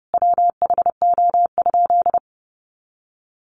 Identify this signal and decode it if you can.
Morse code, heard as a keyed tone, about 28 wpm, 700 Hz. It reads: WHO?
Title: Sentences from Top 800 Words 30wpm